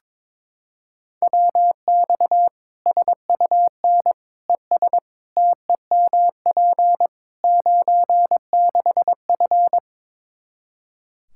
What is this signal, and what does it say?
Morse code, heard as a keyed tone, about 21 wpm, 700 Hz. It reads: WX SUN ES TEMP 96F
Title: Practice QSO - Exchange 2 - Info - 22wpm